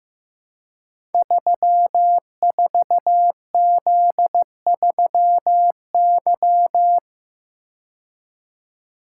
Morse code, heard as a keyed tone, about 15 wpm, 700 Hz. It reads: 34Z3Y
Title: Code Groups - 5 Letter:Numbers - 15wpm